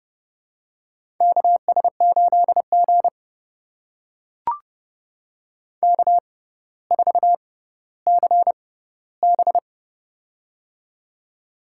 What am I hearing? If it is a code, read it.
Morse code, heard as a keyed tone, about 30 wpm, spaced as for 15 wpm, 700 Hz. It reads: KS8G K4CB
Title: USA Call Signs - 4 Characters - 6x Character Spacing - 30wpm